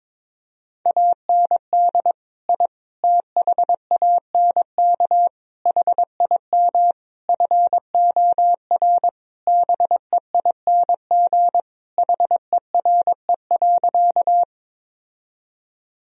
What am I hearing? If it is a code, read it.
Morse code, heard as a keyed tone, about 22 wpm, 700 Hz. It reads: AND I THANK HIM FOR BEING HERE.
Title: Sentences from Top 1000 Words - Review Part 3 - 22wpm